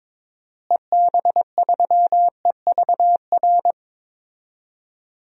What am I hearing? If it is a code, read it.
Morse code, heard as a keyed tone, about 21 wpm, 700 Hz. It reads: EB3EVR
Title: DXCC-100 Call Signs - 6 Characters - Repeated 3x - 22wpm